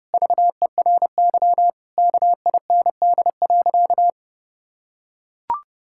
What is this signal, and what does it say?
Morse code, heard as a keyed tone, about 30 wpm, 700 Hz. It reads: VERY KIND.
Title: Sentences from Top 1600 Words 30wpm